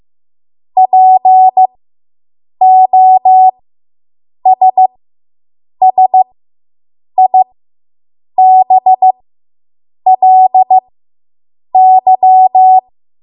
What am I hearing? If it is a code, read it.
Morse code, heard as a keyed tone, about 15 wpm, spaced as for 7 wpm, 750 Hz. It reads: POSSIBLY